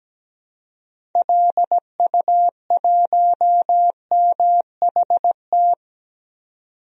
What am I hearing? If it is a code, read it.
Morse code, heard as a keyed tone, about 17 wpm, 700 Hz. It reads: LU1MHT
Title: DXCC-100 Call Signs - 6 Characters - 17wpm